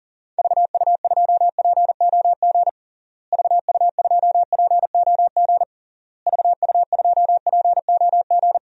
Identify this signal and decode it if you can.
Morse code, heard as a keyed tone, about 40 wpm, 700 Hz. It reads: VU2POG VU2POG VU2POG
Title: DXCC-100 Call Signs - 6 Characters - Repeated 3x - 40wpm